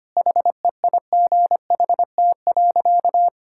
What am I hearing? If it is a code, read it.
Morse code, heard as a keyed tone, about 25 wpm, 700 Hz. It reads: HEIGHT.
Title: Sentences from Top 1600 Words 25wpm